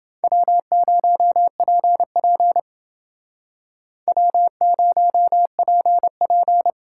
Morse code, heard as a keyed tone, about 29 wpm, 700 Hz. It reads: W0PP W0PP